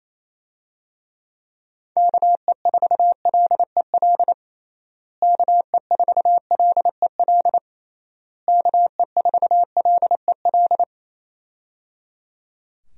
Morse code, heard as a keyed tone, about 28 wpm, 700 Hz. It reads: KE4LEL KE4LEL KE4LEL
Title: USA Call Signs - 6 Characters - Repeated 3x - 28wpm